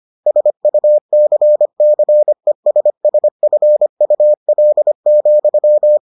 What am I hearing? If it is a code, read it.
Morse code, heard as a keyed tone, about 25 wpm, 600 Hz. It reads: SUCCESSFUL,